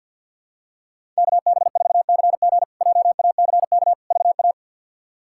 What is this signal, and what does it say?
Morse code, heard as a keyed tone, about 48 wpm, 700 Hz. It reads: KB4CG JACK VA
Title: K1USN SST QSO Practice - 50wpm